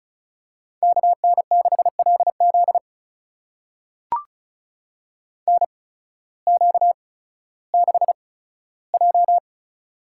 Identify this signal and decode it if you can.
Morse code, heard as a keyed tone, about 36 wpm, spaced as for 16 wpm, 700 Hz. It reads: KN6LZ NQ6J
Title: USA Call Signs - 8x Character Spacing - 35wpm